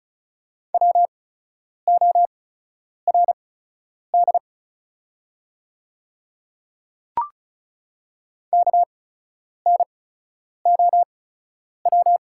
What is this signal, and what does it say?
Morse code, heard as a keyed tone, about 36 wpm, spaced as for 10 wpm, 700 Hz. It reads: WORD KNOW
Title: 8x Character Spacing - Top 500 Words - 35wpm